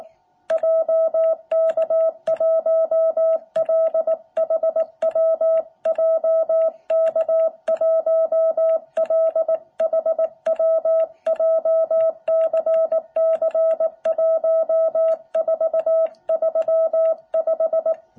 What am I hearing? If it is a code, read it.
Morse code, heard as a keyed tone, about 18 wpm, 660 Hz. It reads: JK1LHW JK1LHW JCC1435
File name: jh8cbh / 2021cw